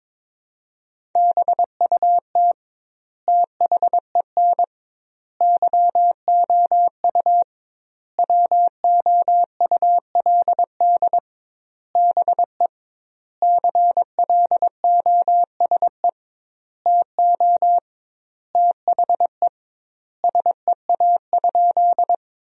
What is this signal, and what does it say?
Morse code, heard as a keyed tone, about 23 wpm, 700 Hz. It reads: BUT THEN YOU WOULD BE CLOSE TO THE SEA?
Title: 2x Word-Spacing - Sentences from Top 300 Words - 22wpm